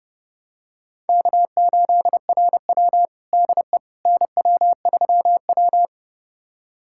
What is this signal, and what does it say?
Morse code, heard as a keyed tone, about 30 wpm, 700 Hz. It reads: K8RW DE NW3W
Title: USA Callsign DE Callsign - 4 Characters - 30wpm